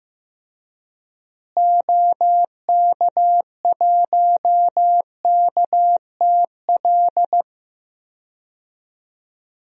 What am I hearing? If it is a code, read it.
Morse code, heard as a keyed tone, about 15 wpm, 700 Hz. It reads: OK1KTL
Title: DXCC-100 Call Signs - 6 Characters - Repeated 2x - 15wpm